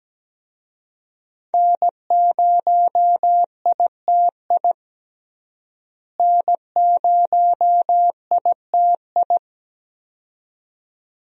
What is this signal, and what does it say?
Morse code, heard as a keyed tone, about 17 wpm, 700 Hz. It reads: N0ITI N0ITI
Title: USA Call Signs - 5 Characters - Repeated 2x - 17wpm